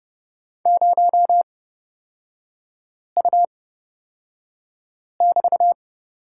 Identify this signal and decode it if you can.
Morse code, heard as a keyed tone, about 29 wpm, spaced as for 5 wpm, 700 Hz. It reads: 0U<BT>
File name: Lesson 36e - BT - Three Characters 30wpm